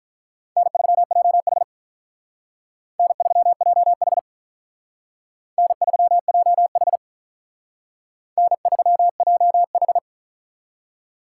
Morse code, heard as a keyed tone, about 44 wpm, 700 Hz. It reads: N3JH N3JH N3JH N3JH